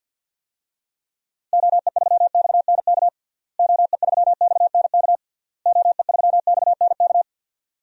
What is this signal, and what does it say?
Morse code, heard as a keyed tone, about 51 wpm, 700 Hz. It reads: OE3XNK OE3XNK OE3XNK
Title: DXCC-100 Call Signs - 6 Characters - Repeated 3x - 50wpm